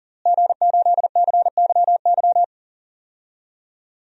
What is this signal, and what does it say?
Morse code, heard as a keyed tone, about 40 wpm, 700 Hz. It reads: G8CYY
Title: DXCC-100 Call Signs - 5 Characters - Repeated 3x - 40wpm